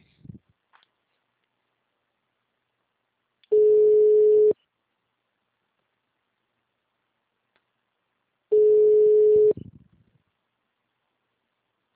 Ringback tone (Europe, 425 Hz).